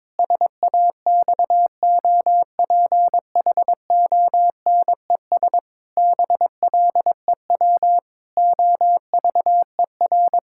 Morse code, heard as a keyed tone, about 22 wpm, 700 Hz. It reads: SAXOPHONES BLEW OVER
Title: Pangram Challenge v2 - 22wpm